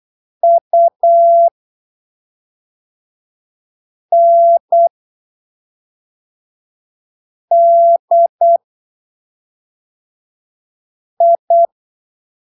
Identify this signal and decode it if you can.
Morse code, heard as a keyed tone, about 8 wpm, spaced as for 3 wpm, 680 Hz. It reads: UNDI
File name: arquivos / texto / texto 14 08ppm